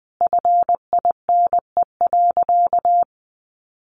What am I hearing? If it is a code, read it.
Morse code, heard as a keyed tone, about 20 wpm, 700 Hz. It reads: FINE.
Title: Sentences from Top 1200 Words 20wpm